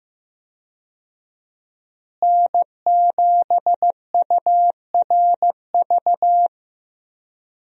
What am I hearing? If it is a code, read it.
Morse code, heard as a keyed tone, about 16 wpm, 700 Hz. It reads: N7URV
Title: USA Call Signs - 5 Characters - Repeated 3x - 15wpm